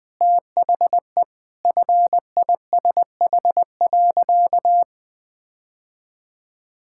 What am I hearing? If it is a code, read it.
Morse code, heard as a keyed tone, about 19 wpm, 700 Hz. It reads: THE FISH.
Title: Sentences with 8-Word Phrases - Spoken Beginning - 20wpm